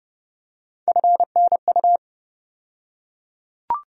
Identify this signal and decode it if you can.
Morse code, heard as a keyed tone, about 32 wpm, 700 Hz. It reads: FNU
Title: USA Call Sign Postfixes - 3 Letters 30wpm